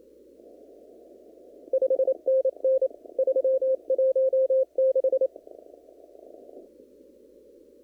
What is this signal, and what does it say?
Morse code, heard as a keyed tone, about 28 wpm, 530 Hz. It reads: 5NN 316
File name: CW / kt5j wpx cw-op-k5pi